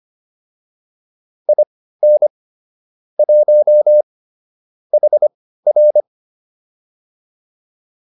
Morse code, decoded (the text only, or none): IN 1 HR